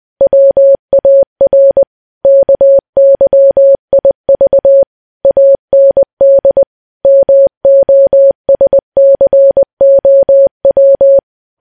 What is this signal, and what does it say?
Morse code, heard as a keyed tone, about 20 wpm, 550 Hz. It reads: WAR KYIV AND MOSCOW
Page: news-headlines-in-morse-code-at-20-wpm